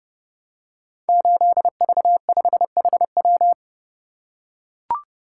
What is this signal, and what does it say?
Morse code, heard as a keyed tone, about 30 wpm, 700 Hz. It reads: 8V5HW
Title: Code Groups - 5 Letter:Numbers - 30wpm